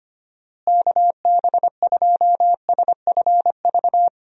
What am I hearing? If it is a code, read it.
Morse code, heard as a keyed tone, about 25 wpm, 700 Hz. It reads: KB2SFV
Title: USA Call Signs - 6 Characters - Repeated 2x - 25wpm